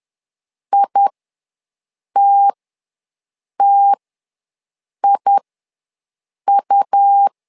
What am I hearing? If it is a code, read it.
Morse code, heard as a keyed tone, about 11 wpm, spaced as for 6 wpm, 800 Hz. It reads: ITTIU
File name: LEZIONI / 2 UHDN / 2 U ws 10wpm 2017